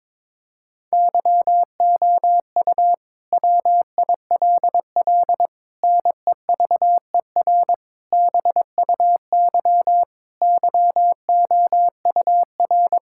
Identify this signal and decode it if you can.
Morse code, heard as a keyed tone, about 22 wpm, 700 Hz. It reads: YOU WILL NEVER BUY YOUR
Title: Sentences from Top 1000 Words - Review Part 5 - 22wpm